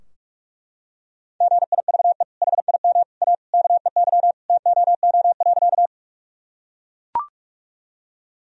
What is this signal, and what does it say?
Morse code, heard as a keyed tone, about 41 wpm, 700 Hz. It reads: GIVE HIM A KEY TOO.
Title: Sentences from Top 1000 Words - Review Part 4 - 45wpm